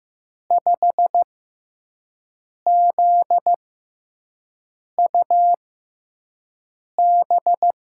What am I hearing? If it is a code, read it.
Morse code, heard as a keyed tone, about 15 wpm, spaced as for 5 wpm, 700 Hz. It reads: 5ZUB
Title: USA Call Signs - 6x Character Spacing - 15wpm